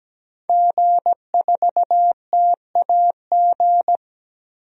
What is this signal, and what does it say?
Morse code, heard as a keyed tone, about 17 wpm, 700 Hz. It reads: G4TAG
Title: DXCC-100 Call Signs - 5 Characters - 17wpm